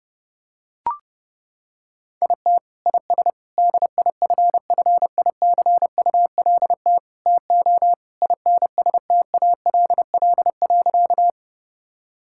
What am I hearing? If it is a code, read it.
Morse code, heard as a keyed tone, about 30 wpm, 700 Hz. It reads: IT IS DIFFICULT TO INSTALL.
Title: Sentences from Top 1600 Words 30wpm